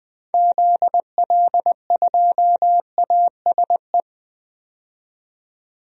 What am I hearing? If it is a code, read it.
Morse code, heard as a keyed tone, about 20 wpm, 700 Hz. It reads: ZL2ASE
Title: DXCC-100 Call Signs - 6 Characters - 20wpm